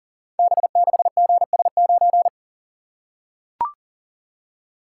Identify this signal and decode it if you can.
Morse code, heard as a keyed tone, about 39 wpm, 690 Hz. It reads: B6GS9
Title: Code Groups - 5 Letter:Numbers - 40wpm